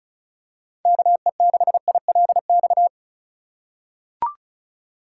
Morse code, heard as a keyed tone, about 36 wpm, 700 Hz. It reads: KE6ILX
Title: USA Call Signs - 3x Character Spacing - 35wpm